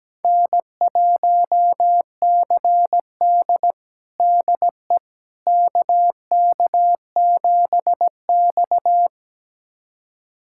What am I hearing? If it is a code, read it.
Morse code, heard as a keyed tone, about 17 wpm, 700 Hz. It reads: N1CD DE KK7X
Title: USA Callsign DE Callsign - 4 Characters - 17wpm